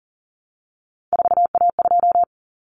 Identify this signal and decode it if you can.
Morse code, heard as a keyed tone, about 40 wpm, 700 Hz. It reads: 4A2
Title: Common Cut Numbers - 3-Digit 40wpm